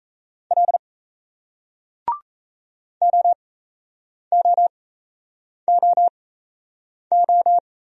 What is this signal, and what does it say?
Morse code, heard as a keyed tone, about 38 wpm, spaced as for 8 wpm, 700 Hz. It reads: L OOOO